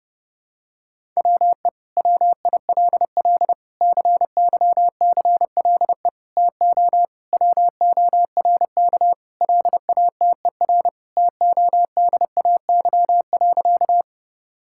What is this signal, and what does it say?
Morse code, heard as a keyed tone, about 30 wpm, 700 Hz. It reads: WE WILL CYCLE TO WORK LATER TODAY.
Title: Sentences from Top 1000 Words - Review Part 6 - 30wpm